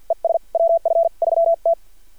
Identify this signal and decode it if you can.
Morse code, heard as a keyed tone, about 51 wpm, spaced as for 41 wpm, 670 Hz. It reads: ES WF3T
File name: archived / CRSnet